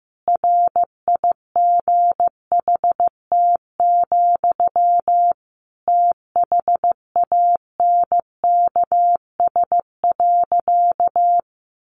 Morse code, heard as a keyed tone, about 15 wpm, 700 Hz. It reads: RIGHT, THANKS.
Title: Sentences from Top 600 Words 15wpm